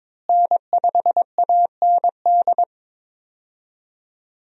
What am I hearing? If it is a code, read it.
Morse code, heard as a keyed tone, about 22 wpm, 700 Hz. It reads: N5AND
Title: USA Call Signs - 5 Characters - Repeated 2x - 22wpm